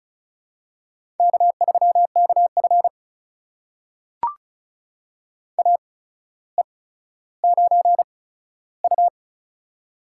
Morse code, heard as a keyed tone, about 36 wpm, spaced as for 15 wpm, 700 Hz. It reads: K3KF AE9U
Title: USA Call Signs - 4 Characters - 8x Character Spacing - 35wpm